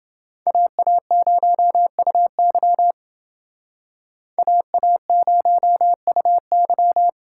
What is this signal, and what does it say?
Morse code, heard as a keyed tone, about 28 wpm, 700 Hz. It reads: AA0UY AA0UY